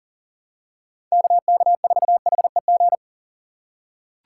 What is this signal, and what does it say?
Morse code, heard as a keyed tone, about 40 wpm, 700 Hz. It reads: KK4HEG
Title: K1USN SST QSO Practice - 40wpm